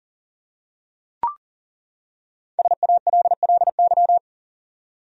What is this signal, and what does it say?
Morse code, heard as a keyed tone, about 37 wpm, 700 Hz. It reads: SALLY
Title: USA Common Names - Max 5 Letters - 40wpm